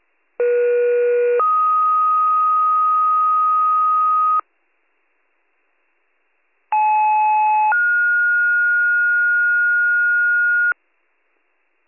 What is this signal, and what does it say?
Two-tone page, heard as a one-second tone, then a three-second tone, about 480 then 1250 Hz, about 860 then 1430 Hz. It is heard from 0.4 to 10.7 s.